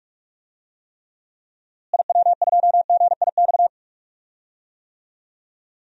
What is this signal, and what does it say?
Morse code, heard as a keyed tone, about 41 wpm, 700 Hz. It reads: IW2GIX
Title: DXCC-100 Call Signs - 6 Characters - Repeated 2x - 45wpm